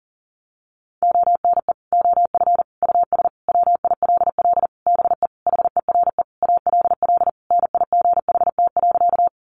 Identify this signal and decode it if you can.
Morse code, heard as a keyed tone, about 40 wpm, 700 Hz. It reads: ONE OF US WILL BE HERE ALL NIGHT.
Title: Sentences from Top 300 Words 40wpm